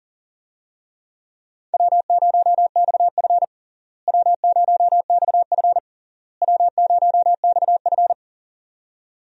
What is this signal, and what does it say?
Morse code, heard as a keyed tone, about 37 wpm, 700 Hz. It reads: W0XF W0XF W0XF
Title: USA Call Signs - 4 Characters - Repeated 3x - 40wpm